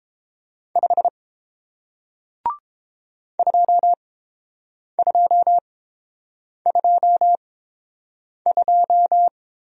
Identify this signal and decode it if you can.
Morse code, heard as a keyed tone, about 30 wpm, spaced as for 7 wpm, 700 Hz. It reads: 5 2222